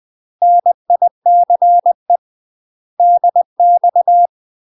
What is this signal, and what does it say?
Morse code, heard as a keyed tone, about 20 wpm, 700 Hz. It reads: NICE DX
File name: UK MASTER QSOs TRAINING / 20WPM / UK MASTER QSOs 0012